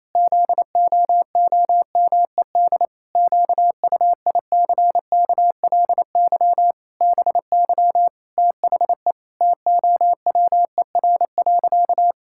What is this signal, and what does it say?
Morse code, heard as a keyed tone, about 27 wpm, 700 Hz. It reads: ZOOMED QUICKLY BY THE TOWER.
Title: Pangram Challenge v2 - 28wpm